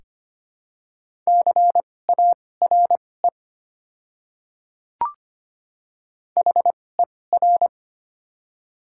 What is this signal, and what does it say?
Morse code, heard as a keyed tone, about 25 wpm, spaced as for 18 wpm, 700 Hz. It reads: CARE HER
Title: 2x Character Spacing - Top 500 Words - 25wpm